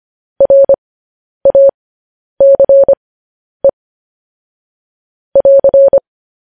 Morse code, heard as a keyed tone, about 25 wpm, spaced as for 10 wpm, 550 Hz. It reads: RACE <AR>
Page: News Headlines in Morse Code at 10 WPM